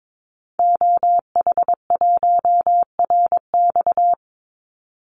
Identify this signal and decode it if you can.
Morse code, heard as a keyed tone, about 22 wpm, 700 Hz. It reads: OH1RX
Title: Call Signs - 22wpm